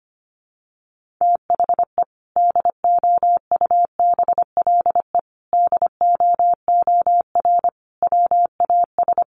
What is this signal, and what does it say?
Morse code, heard as a keyed tone, about 25 wpm, 700 Hz. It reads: THE DOUBLE DOOR WAS
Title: Sentences from Top 1400 Words 25wpm